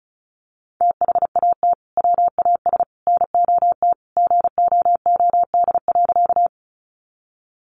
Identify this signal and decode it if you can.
Morse code, heard as a keyed tone, about 34 wpm, 700 Hz. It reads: THAT WAS NOT GOOD.
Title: Sentences from Top 600 Words 35wpm